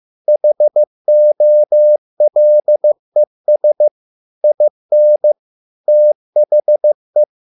Morse code, heard as a keyed tone, about 15 wpm, 600 Hz. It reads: HOLES IN THE